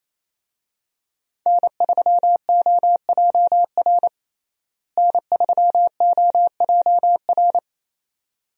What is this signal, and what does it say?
Morse code, heard as a keyed tone, about 27 wpm, 700 Hz. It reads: N3OJR N3OJR
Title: USA Call Signs - 5 Characters - Repeated 2x - 28wpm